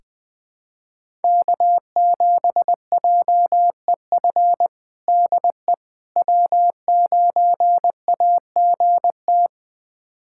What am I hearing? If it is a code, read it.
Morse code, heard as a keyed tone, about 20 wpm, 700 Hz. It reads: K7JEF DE W9AGT
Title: USA Callsign DE Callsign - 5 Characters - 20wpm